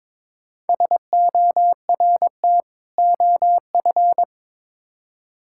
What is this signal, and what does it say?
Morse code, heard as a keyed tone, about 22 wpm, 700 Hz. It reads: SORT OF
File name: Build A Phrase - 2-Word Phrases - 22wpm